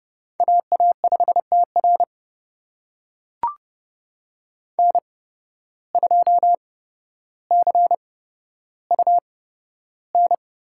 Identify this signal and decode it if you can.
Morse code, heard as a keyed tone, about 30 wpm, spaced as for 13 wpm, 700 Hz. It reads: AA5TR N2CUN
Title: USA Call Signs - 5 Characters - 8x Character Spacing - 30wpm